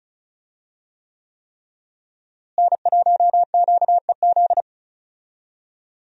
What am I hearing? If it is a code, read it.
Morse code, heard as a keyed tone, about 36 wpm, 700 Hz. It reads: N1QEZ
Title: USA Call Signs - 5 Characters - Repeated 3x - 35wpm